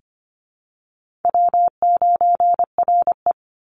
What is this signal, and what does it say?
Morse code, heard as a keyed tone, about 25 wpm, 700 Hz. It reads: W9RE
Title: Call Signs Repeated 2x - 25wpm